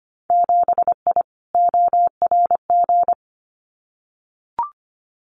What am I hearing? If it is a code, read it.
Morse code, heard as a keyed tone, about 25 wpm, 700 Hz. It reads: 7I ORG
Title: Winter Field Day Exchanges Encore 25wpm